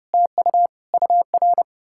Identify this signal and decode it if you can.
Morse code, heard as a keyed tone, about 40 wpm, 700 Hz. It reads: TU UR